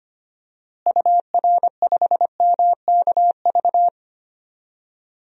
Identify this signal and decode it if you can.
Morse code, heard as a keyed tone, about 25 wpm, 700 Hz. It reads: UR5MKV
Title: DXCC-100 Call Signs - 6 Characters - Repeated 2x - 25wpm